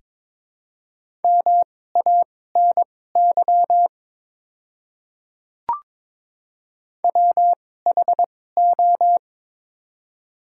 Morse code, heard as a keyed tone, about 22 wpm, spaced as for 16 wpm, 700 Hz. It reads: MANY WHO
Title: 2x Character Spacing - Top 300 Words - 22wpm